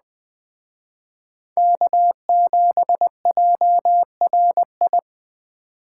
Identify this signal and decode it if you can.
Morse code, heard as a keyed tone, about 19 wpm, 700 Hz. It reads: K7JRI